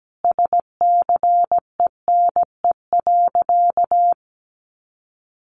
Morse code, heard as a keyed tone, about 17 wpm, 700 Hz. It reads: SCENE.
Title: Sentences from Top 1100 Words 17wpm